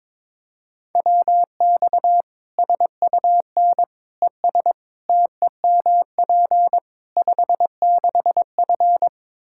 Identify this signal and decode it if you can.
Morse code, heard as a keyed tone, about 22 wpm, 700 Hz. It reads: WX SUN ES TEMP 56F
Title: Practice QSO - Exchange 2 - Info - 22wpm